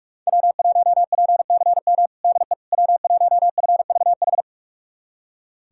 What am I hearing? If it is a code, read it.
Morse code, heard as a keyed tone, about 44 wpm, 700 Hz. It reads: W1PCM DE W1FVH
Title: USA Callsign DE Callsign - 5 Characters - 45wpm